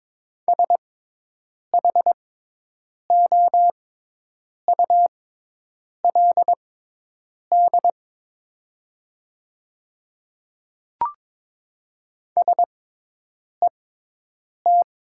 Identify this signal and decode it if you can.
Morse code, heard as a keyed tone, about 22 wpm, spaced as for 8 wpm, 700 Hz. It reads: SHOULD SET
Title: 6x Character Spacing - Top 300 Words - 22wpm